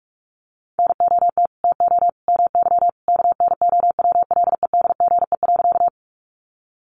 Morse code, heard as a keyed tone, about 44 wpm, 700 Hz. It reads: NOT TO MY KNOWLEDGE.
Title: Sentences from Top 600 Words 45wpm